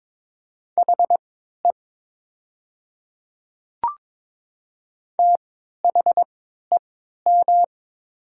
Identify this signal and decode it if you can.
Morse code, heard as a keyed tone, about 22 wpm, spaced as for 12 wpm, 700 Hz. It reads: HE THEM
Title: 3x Character Spacing - Top 100 Words - 22wpm